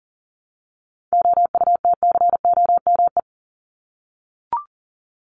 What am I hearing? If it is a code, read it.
Morse code, heard as a keyed tone, about 40 wpm, 700 Hz. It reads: OUTCOME
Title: Top 1501-1750 Words 40wpm